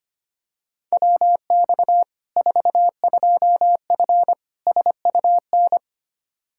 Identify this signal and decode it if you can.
Morse code, heard as a keyed tone, about 25 wpm, 700 Hz. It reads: WX 42F SUN
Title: K1USN SST QSO Practice 25wpm